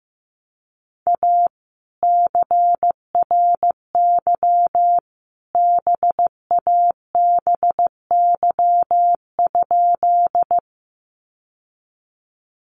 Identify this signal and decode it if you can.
Morse code, heard as a keyed tone, about 15 wpm, 700 Hz. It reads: A CRY BABY?
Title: Sentences from Top 1300 Words 15wpm